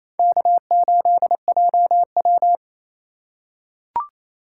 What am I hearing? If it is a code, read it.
Morse code, heard as a keyed tone, about 27 wpm, 700 Hz. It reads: K8JW
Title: USA Call Signs - 4 Characters - 3x Character Spacing - 28wpm